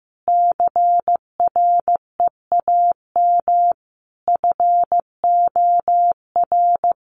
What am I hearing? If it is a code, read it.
Morse code, heard as a keyed tone, about 15 wpm, 700 Hz. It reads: CREAM FOR
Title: Sentences from Top 1500 Words 15wpm